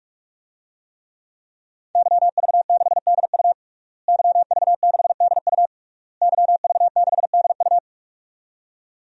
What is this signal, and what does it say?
Morse code, heard as a keyed tone, about 48 wpm, 700 Hz. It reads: YV6DU YV6DU YV6DU
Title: DXCC-100 Call Signs - 5 Characters - Repeated 3x - 45wpm